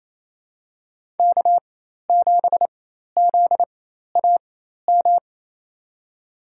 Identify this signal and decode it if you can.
Morse code, heard as a keyed tone, about 28 wpm, spaced as for 13 wpm, 700 Hz. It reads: K7ZAM
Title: USA Call Signs - 5 Characters - 4x Character Spacing - 28wpm